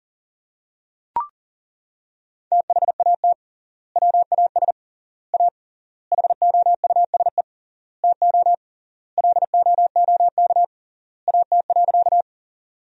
Morse code, heard as a keyed tone, about 40 wpm, 700 Hz. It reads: THAT WAS A HOUSE TO LOOK AT.
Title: 3x Word-Spacing - Sentences from Top 200 Words - 40wpm